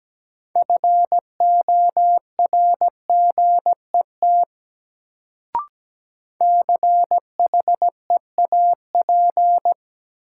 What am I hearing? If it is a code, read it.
Morse code, heard as a keyed tone, about 17 wpm, 700 Hz. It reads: FORGET CHEAP
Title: Sentences from Top 1200 Words - New Word Drill - 17wpm